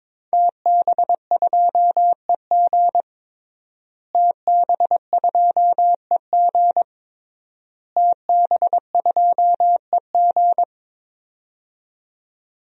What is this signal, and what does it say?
Morse code, heard as a keyed tone, about 21 wpm, 700 Hz. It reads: TB2EG TB2EG TB2EG
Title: DXCC-100 Call Signs - 5 Characters - Repeated 3x - 22wpm